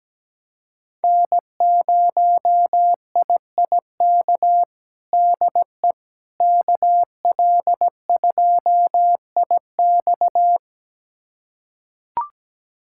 Morse code, heard as a keyed tone, about 17 wpm, 700 Hz. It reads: N0IIK DE KL2IX
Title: USA Callsign DE Callsign - 5 Characters - 17wpm